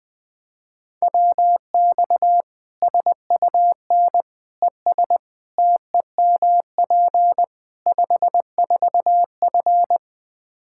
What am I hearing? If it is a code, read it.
Morse code, heard as a keyed tone, about 20 wpm, 700 Hz. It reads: WX SUN ES TEMP 54F
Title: Practice QSO - Exchange 2 - Info - 20wpm